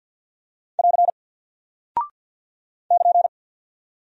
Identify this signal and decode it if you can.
Morse code, heard as a keyed tone, about 48 wpm, 700 Hz. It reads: <AR> <KN>
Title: Single Character (Letter:Number:Punctuation:Prosign) Alt 50wpm